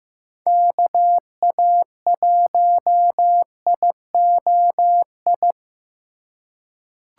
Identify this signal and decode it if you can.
Morse code, heard as a keyed tone, about 15 wpm, 700 Hz. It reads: KA1IOI
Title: USA Call Signs - 6 Characters - 2x Character Spacing - 15wpm